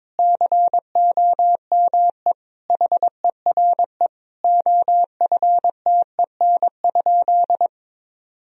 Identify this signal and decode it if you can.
Morse code, heard as a keyed tone, about 22 wpm, 700 Hz. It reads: COME HERE OFTEN?
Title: Sentences from Top 1000 Words - Review Part 1 - 22wpm